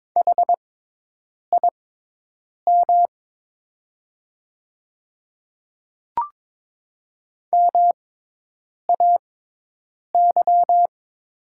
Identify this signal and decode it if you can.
Morse code, heard as a keyed tone, about 22 wpm, spaced as for 8 wpm, 700 Hz. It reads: HIM MAY
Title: 6x Character Spacing - Top 100 Words - 22wpm